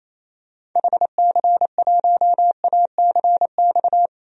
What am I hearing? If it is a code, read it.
Morse code, heard as a keyed tone, about 27 wpm, 700 Hz. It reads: HC1ACX
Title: DXCC-100 Call Signs - 6 Characters - 28wpm